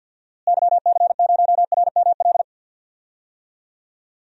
Morse code, heard as a keyed tone, about 46 wpm, 700 Hz. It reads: YC0RML
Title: DXCC-100 Call Signs - 6 Characters - Repeated 3x - 50wpm